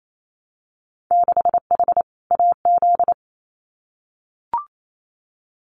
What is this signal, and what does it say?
Morse code, heard as a keyed tone, about 28 wpm, 700 Hz. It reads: 6H AZ
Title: Winter Field Day Exchanges Encore 28wpm